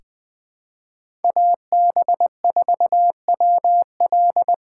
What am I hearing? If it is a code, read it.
Morse code, heard as a keyed tone, about 21 wpm, 700 Hz. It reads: AB4WL
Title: 2023 SOTA Chaser Call Signs - 20wpm